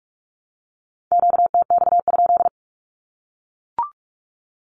Morse code, heard as a keyed tone, about 44 wpm, 700 Hz. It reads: QTX?
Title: Q-Codes Repeated 3x 45wpm